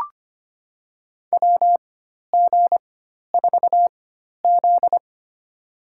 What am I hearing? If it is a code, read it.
Morse code, heard as a keyed tone, about 25 wpm, spaced as for 12 wpm, 700 Hz. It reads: WG4Z
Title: USA Call Signs - 4 Characters - 4x Character Spacing - 25wpm